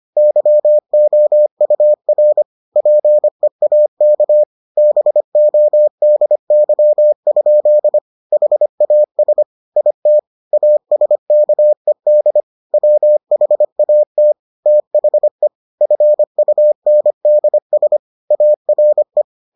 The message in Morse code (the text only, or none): YOUR PEAK BODY? HAS IT ASKED WHAT THE FUNDS ARE